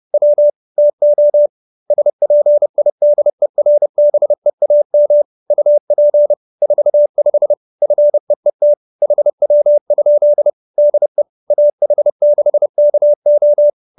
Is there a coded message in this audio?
Yes: Morse code from 0.1 to 13.7 s, heard as a keyed tone, about 30 wpm, 590 Hz. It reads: W TO SPIDERBEAM UP 45 FEET HW? DE AH6KO